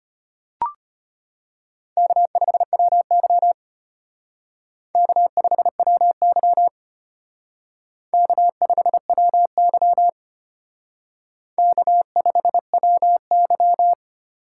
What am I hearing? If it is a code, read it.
Morse code, heard as a keyed tone, about 32 wpm, 700 Hz. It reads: K5WY K5WY K5WY K5WY